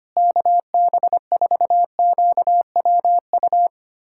Morse code, heard as a keyed tone, about 25 wpm, 700 Hz. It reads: KB4QWU
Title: USA Callsign DE Callsign - 6 Characters - 25wpm